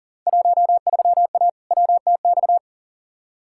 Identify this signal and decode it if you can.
Morse code, heard as a keyed tone, about 40 wpm, 700 Hz. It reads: 13A WTX
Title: Field Day Exchanges Encore - 40wpm